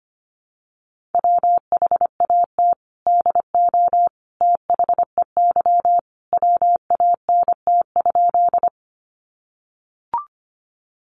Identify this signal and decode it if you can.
Morse code, heard as a keyed tone, about 25 wpm, 700 Hz. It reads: WHAT DO THEY WANT?
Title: Sentences from Top 200 Words 25wpm